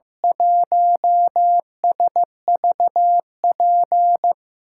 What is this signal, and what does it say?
Morse code, heard as a keyed tone, about 15 wpm, 700 Hz. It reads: 1SVP
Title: USA Call Signs - 6 Characters - 6x Character Spacing - 15wpm